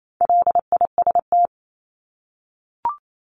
Morse code, heard as a keyed tone, about 28 wpm, 700 Hz. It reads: LIST
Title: Top 500 Words Repeated 28wpm